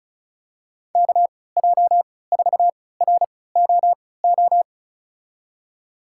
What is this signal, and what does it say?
Morse code, heard as a keyed tone, about 35 wpm, spaced as for 20 wpm, 700 Hz. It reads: KJ4ROO
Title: USA Call Signs - 6 Characters - 3x Character Spacing - 35wpm